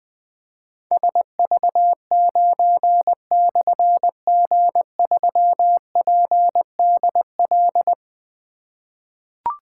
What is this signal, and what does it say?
Morse code, heard as a keyed tone, about 20 wpm, 700 Hz. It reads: SV9/G3PDL
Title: Difficult Call Signs 20wpm